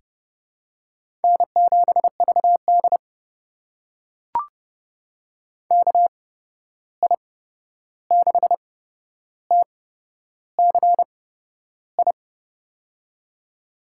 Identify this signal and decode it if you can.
Morse code, heard as a keyed tone, about 30 wpm, spaced as for 11 wpm, 700 Hz. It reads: N7VD KI6TCI
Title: USA Call Signs - 8x Character Spacing - 30wpm